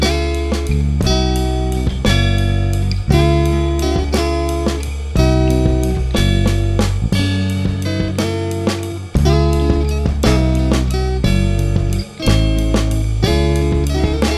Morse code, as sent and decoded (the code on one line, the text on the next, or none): none